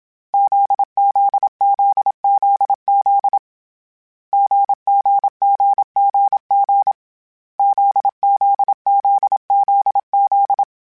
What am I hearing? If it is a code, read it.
Morse code, heard as a keyed tone, about 27 wpm, 800 Hz. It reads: ZZZZZ GGGGG ZZZZZ
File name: CW Code Groups-mp3s / Sets mp3s / GZ GZ7 sp @25